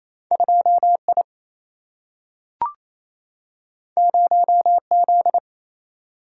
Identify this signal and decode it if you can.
Morse code, heard as a keyed tone, about 28 wpm, 700 Hz. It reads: 2 I 0 Z
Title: Single-Letter Single-Number 28wpm